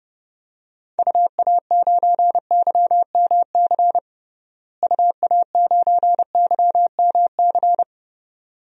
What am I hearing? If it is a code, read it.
Morse code, heard as a keyed tone, about 30 wpm, 700 Hz. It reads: UA9YMC UA9YMC
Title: DXCC-100 Call Signs - 6 Characters - Repeated 2x - 30wpm